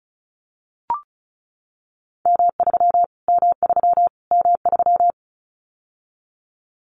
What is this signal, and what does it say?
Morse code, heard as a keyed tone, about 34 wpm, 700 Hz. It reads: M3 M3 M3
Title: Callsign Prefixes for Top 25 DXCC - Repeated 3x - 35wpm